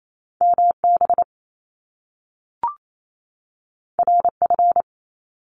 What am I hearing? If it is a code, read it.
Morse code, heard as a keyed tone, about 28 wpm, 700 Hz. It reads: MB RF
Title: Top Two Letter Combinations in Words 28wpm